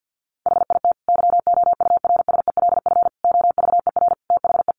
Morse code, heard as a keyed tone, about 50 wpm, 700 Hz. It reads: HIT YOURSELF OVER THE
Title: Sentences from Top 1200 Words 50wpm